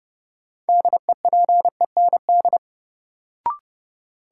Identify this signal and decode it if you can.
Morse code, heard as a keyed tone, about 30 wpm, 700 Hz. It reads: DEPEND
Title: Sentences from Top 1300 Words - New Word Drill - 30wpm